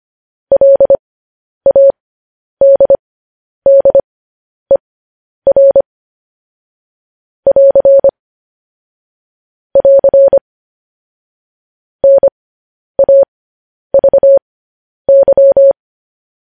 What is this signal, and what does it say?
Morse code, heard as a keyed tone, about 25 wpm, spaced as for 10 wpm, 550 Hz. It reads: LADDER <AR> <AR> NAVY
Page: news-headlines-in-morse-code-at-10-wpm